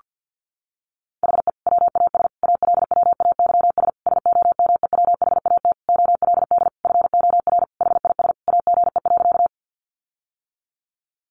Morse code, heard as a keyed tone, about 50 wpm, 700 Hz. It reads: HE WAS ALWAYS SOMEWHAT OLD FOR HIS AGE.
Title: Sentences from Top 1300 Words 50wpm